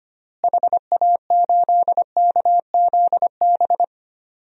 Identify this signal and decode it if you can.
Morse code, heard as a keyed tone, about 25 wpm, 700 Hz. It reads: HA8KZB
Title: DXCC-100 Call Signs - 6 Characters - Repeated 3x - 25wpm